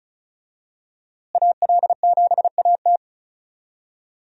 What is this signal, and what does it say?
Morse code, heard as a keyed tone, about 36 wpm, 700 Hz. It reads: AL7AT
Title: USA Call Signs - 5 Characters - Repeated 3x - 35wpm